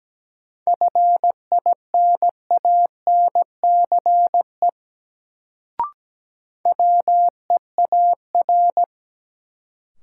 Morse code, heard as a keyed tone, about 17 wpm, 700 Hz. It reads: FINANCE WEAR
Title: Sentences from Top 1500 Words - New Word Drill Alt - 17wpm